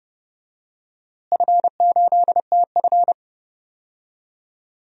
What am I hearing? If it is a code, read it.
Morse code, heard as a keyed tone, about 30 wpm, 700 Hz. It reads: F8TF
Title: DXCC-100 Call Signs - 4 Characters - Repeated 2x - 30wpm